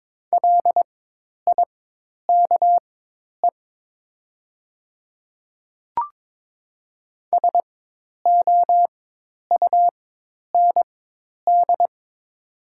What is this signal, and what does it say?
Morse code, heard as a keyed tone, about 22 wpm, spaced as for 10 wpm, 700 Hz. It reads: LIKE SOUND
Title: 4x Character Spacing - Top 200 Words - 22wpm